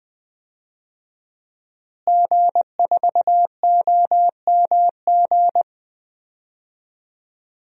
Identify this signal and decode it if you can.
Morse code, heard as a keyed tone, about 20 wpm, 700 Hz. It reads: G4OMG
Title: DXCC-100 Call Signs - 5 Characters - Repeated 3x - 20wpm